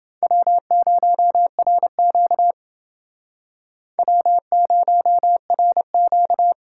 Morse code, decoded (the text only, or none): W0RQ W0RQ